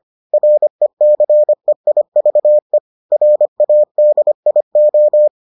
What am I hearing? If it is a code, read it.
Morse code, heard as a keyed tone, about 25 wpm, 600 Hz. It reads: RECEIVE RADIO